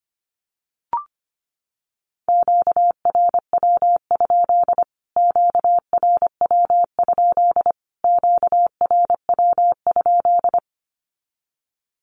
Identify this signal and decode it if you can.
Morse code, heard as a keyed tone, about 25 wpm, 700 Hz. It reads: QRW? QRW? QRW?
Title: Q-Codes Repeated 3x 25wpm